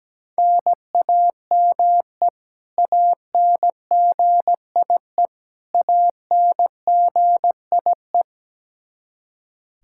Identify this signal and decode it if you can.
Morse code, heard as a keyed tone, about 17 wpm, 700 Hz. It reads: NAME ANGIE ANGIE
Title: K1USN SST QSO Practice 17wpm